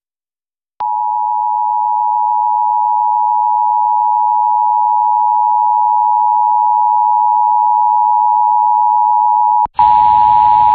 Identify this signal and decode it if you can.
Emergency Alert System (SAME) tones (the 853 and 960 Hz attention tone). It is heard from 0.7 to 9.7 s.